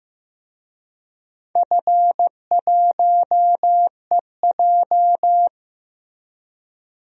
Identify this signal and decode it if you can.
Morse code, heard as a keyed tone, about 15 wpm, 700 Hz. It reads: F1EJ
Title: DXCC-100 Call Signs - 4 Characters - Repeated 2x - 15wpm